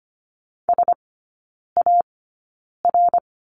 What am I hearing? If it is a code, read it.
Morse code, heard as a keyed tone, about 25 wpm, spaced as for 9 wpm, 690 Hz. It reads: SAR
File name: Lesson 14e - Number 2 - Three Characters 25wpm